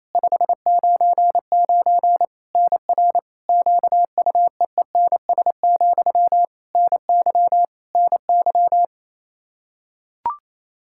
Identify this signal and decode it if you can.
Morse code, heard as a keyed tone, about 28 wpm, 700 Hz. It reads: 599 NR QUEENS, NY NY
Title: K1USN SST QSO Practice 28wpm